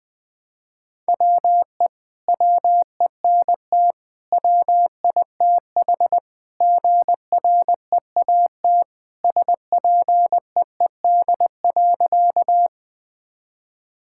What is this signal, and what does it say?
Morse code, heard as a keyed tone, about 20 wpm, 700 Hz. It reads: WE WENT WITH GREAT SPEED.
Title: Sentences from Top 1000 Words - Review Part 5 - 20wpm